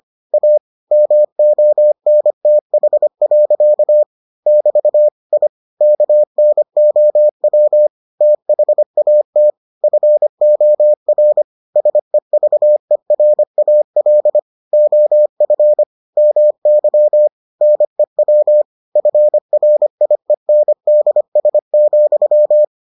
Morse code, heard as a keyed tone, about 25 wpm, 600 Hz. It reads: A MONTH. <BT> I KNOW THAT FOR SEVERAL OF MY NEW FRIENDS,